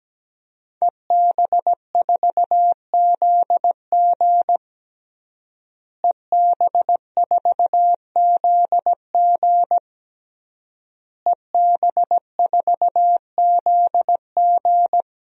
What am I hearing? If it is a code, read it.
Morse code, heard as a keyed tone, about 17 wpm, 700 Hz. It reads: EB4ZG EB4ZG EB4ZG
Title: DXCC-100 Call Signs - 5 Characters - Repeated 3x - 17wpm